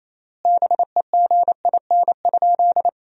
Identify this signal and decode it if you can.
Morse code, heard as a keyed tone, about 28 wpm, 700 Hz. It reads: BEGIN?
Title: 3x Word-Spacing - Sentences from Top 300 Words - 28wpm